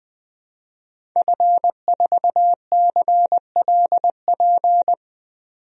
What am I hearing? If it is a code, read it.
Morse code, heard as a keyed tone, about 21 wpm, 700 Hz. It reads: F4CLP
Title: DXCC-100 Call Signs - 5 Characters - 20wpm